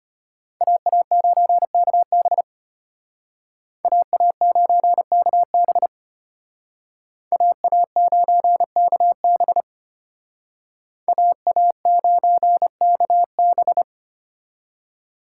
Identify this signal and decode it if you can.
Morse code, heard as a keyed tone, about 32 wpm, 700 Hz. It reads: AA9KB AA9KB AA9KB AA9KB